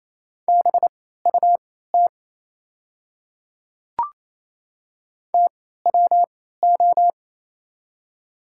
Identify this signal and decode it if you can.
Morse code, heard as a keyed tone, about 28 wpm, spaced as for 16 wpm, 700 Hz. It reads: BUT TWO
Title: 3x Character Spacing - Top 200 Words - 28wpm